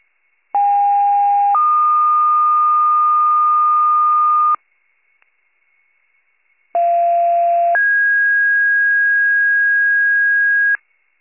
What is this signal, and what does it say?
Two-tone page, heard as a one-second tone, then a three-second tone, about 810 then 1220 Hz, about 690 then 1650 Hz. It is heard from 0.5 to 10.8 s.